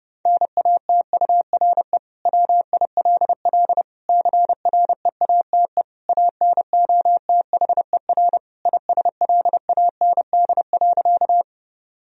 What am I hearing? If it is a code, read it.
Morse code, heard as a keyed tone, about 30 wpm, 700 Hz. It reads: NATURE WILL CREATE ANOTHER ISLAND.
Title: Sentences from Top 1000 Words - Review Part 3 - 30wpm